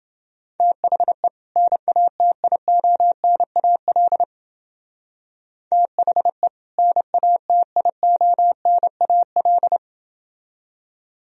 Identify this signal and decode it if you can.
Morse code, heard as a keyed tone, about 28 wpm, 700 Hz. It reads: THE NATIONAL THE NATIONAL